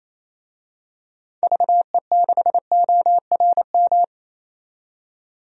Morse code, heard as a keyed tone, about 28 wpm, 700 Hz. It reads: VE6ORM
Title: DXCC-100 Call Signs - 6 Characters - 28wpm